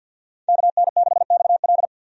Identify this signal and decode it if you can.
Morse code, heard as a keyed tone, about 51 wpm, 700 Hz. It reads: KN6XL
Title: USA Call Signs - 5 Characters - 50wpm